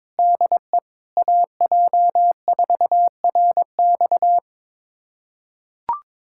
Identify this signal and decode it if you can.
Morse code, heard as a keyed tone, about 22 wpm, 700 Hz. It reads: DE AJ4RX
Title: USA Callsign DE Callsign - 5 Characters - 22wpm